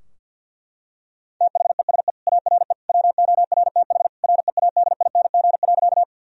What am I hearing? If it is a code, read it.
Morse code, heard as a keyed tone, about 50 wpm, 700 Hz. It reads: THESE ARE WORTH READING.
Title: Sentences from Top 1000 Words - Review Part 3 - 50wpm